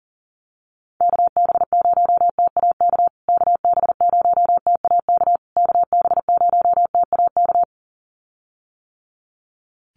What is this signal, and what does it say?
Morse code, heard as a keyed tone, about 40 wpm, 700 Hz. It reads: KB0TAK KB0TAK KB0TAK
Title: USA Call Signs Repeated 3x 40wpm